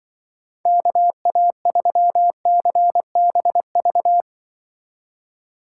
Morse code, heard as a keyed tone, about 24 wpm, 700 Hz. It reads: KA3CBV
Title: Speed Racing - USA 2x3 Call Signs 20wpm